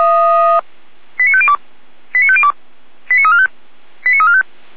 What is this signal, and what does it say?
Five-tone selective call (ZVEI), heard as quick runs of five tones, tones of about 70 ms. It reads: 89502 89502 89345 89345